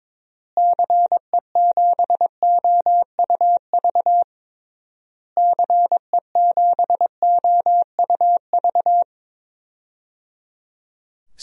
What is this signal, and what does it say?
Morse code, heard as a keyed tone, about 22 wpm, 700 Hz. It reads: CE7OUV CE7OUV
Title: DXCC-100 Call Signs - 6 Characters - Repeated 3x - 22wpm